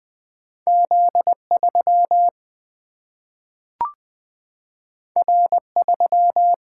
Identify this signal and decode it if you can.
Morse code, heard as a keyed tone, about 20 wpm, 700 Hz. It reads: Z 3 R 3
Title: Single-Letter Single-Number 20wpm